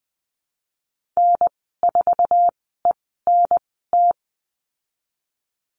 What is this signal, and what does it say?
Morse code, heard as a keyed tone, about 20 wpm, 700 Hz. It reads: N 4 E N T
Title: 055 Lesson 10g - Letter r - Call Signs 20wpm